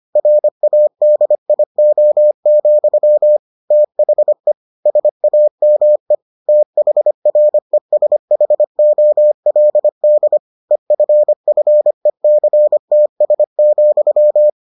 Morse code, heard as a keyed tone, about 25 wpm, 600 Hz. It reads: RADIO, THE SAME THRESHOLD EFFECTS,